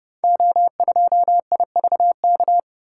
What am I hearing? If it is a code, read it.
Morse code, heard as a keyed tone, about 30 wpm, 700 Hz. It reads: O2IVK
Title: Code Groups - 5 Letter:Numbers - 30wpm